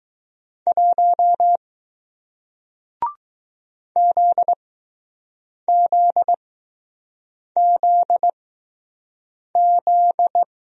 Morse code, heard as a keyed tone, about 20 wpm, spaced as for 6 wpm, 700 Hz. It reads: 1 ZZZZ